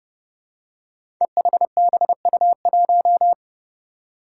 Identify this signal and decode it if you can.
Morse code, heard as a keyed tone, about 30 wpm, 700 Hz. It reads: EHBU1
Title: Five Challenging Letter-Number Combinations 30wpm